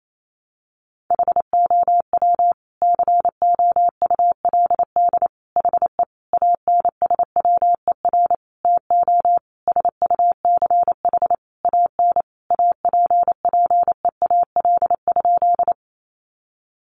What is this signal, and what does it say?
Morse code, heard as a keyed tone, about 28 wpm, 700 Hz. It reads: HOW COULD HE ANSWER TO SUCH AN APPEAL?
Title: Sentences from Top 1300 Words 28wpm